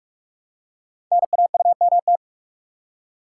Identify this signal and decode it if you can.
Morse code, heard as a keyed tone, about 44 wpm, 690 Hz. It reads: NAUMT
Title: Code Groups - 5 Letters - 45wpm